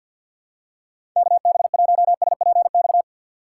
Morse code, heard as a keyed tone, about 48 wpm, 700 Hz. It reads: KB1SPX
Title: USA Call Signs - 6 Characters - 6x Character Spacing - 50wpm